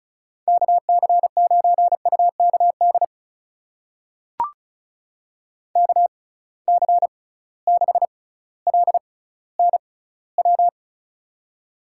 Morse code, heard as a keyed tone, about 34 wpm, spaced as for 18 wpm, 700 Hz. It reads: KC9UKD KC6LNW
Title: USA Call Signs - 6 Characters - 6x Character Spacing - 35wpm